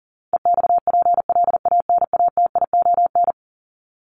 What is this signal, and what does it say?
Morse code, heard as a keyed tone, about 40 wpm, 700 Hz. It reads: EXPLANATION
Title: Top 1751-2000 Words 40wpm